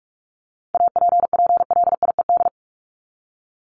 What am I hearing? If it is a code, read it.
Morse code, heard as a keyed tone, about 44 wpm, 700 Hz. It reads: APPLIED
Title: ED Suffix - Top 100 Words with Base Words 45wpm